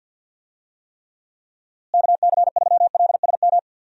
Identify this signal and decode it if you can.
Morse code, heard as a keyed tone, about 47 wpm, 700 Hz. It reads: KC3LSM
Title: USA Call Signs - 6 Characters - Repeated 2x - 50wpm